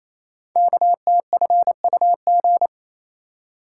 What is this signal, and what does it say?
Morse code, heard as a keyed tone, about 27 wpm, 700 Hz. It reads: KTFUG
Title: Code Groups - 5 Letters - 28wpm